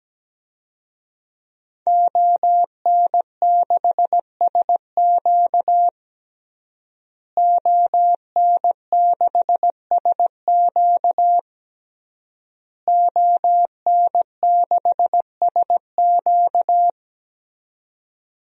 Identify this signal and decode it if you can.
Morse code, heard as a keyed tone, about 17 wpm, 700 Hz. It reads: ON6SQ ON6SQ ON6SQ
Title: DXCC-100 Call Signs - 5 Characters - Repeated 3x - 17wpm